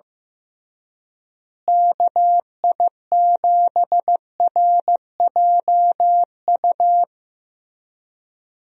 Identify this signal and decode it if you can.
Morse code, heard as a keyed tone, about 15 wpm, 700 Hz. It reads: KI7RJU
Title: USA Call Signs - 6 Characters - Repeated 3x - 15wpm